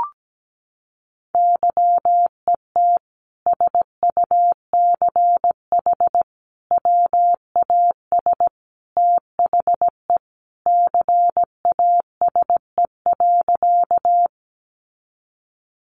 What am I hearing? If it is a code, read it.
Morse code, heard as a keyed tone, about 17 wpm, 700 Hz. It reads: YET SUCH WAS THE CASE.
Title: Sentences from Top 700 Words 17wpm